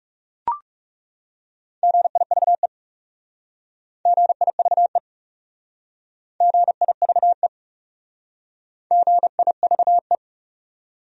Morse code, heard as a keyed tone, about 39 wpm, 700 Hz. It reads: GIVE GIVE GIVE GIVE